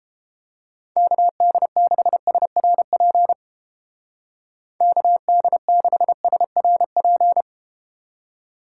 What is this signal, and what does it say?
Morse code, heard as a keyed tone, about 31 wpm, 700 Hz. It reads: KD6SRP KD6SRP